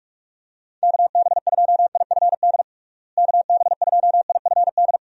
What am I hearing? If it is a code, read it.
Morse code, heard as a keyed tone, about 46 wpm, 700 Hz. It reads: KB2IFD KB2IFD
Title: USA Call Signs - 6 Characters - Repeated 2x - 45wpm